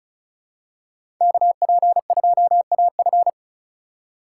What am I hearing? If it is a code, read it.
Morse code, heard as a keyed tone, about 34 wpm, 700 Hz. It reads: KP2AF
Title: USA Call Signs - 5 Characters - 2x Character Spacing - 35wpm